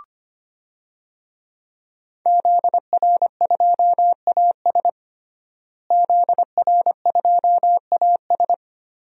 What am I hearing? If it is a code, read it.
Morse code, heard as a keyed tone, about 25 wpm, 700 Hz. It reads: ZR2AS ZR2AS
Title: DXCC-100 Call Signs - 5 Characters - Repeated 3x - 25wpm